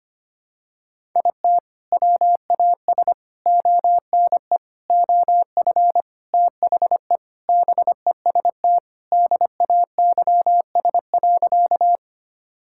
Morse code, heard as a keyed tone, about 25 wpm, 700 Hz. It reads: IT WAS ONE OF THE BEST DAYS.
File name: practice / 25wpm / Sentences with 3-Word Phrases - 25wpm